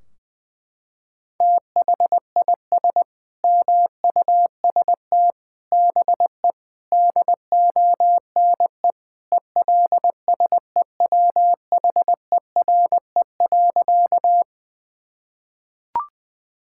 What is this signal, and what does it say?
Morse code, heard as a keyed tone, about 20 wpm, 700 Hz. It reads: THIS MUST BE DONE ELSEWHERE.
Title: Sentences from Top 1600 Words 20wpm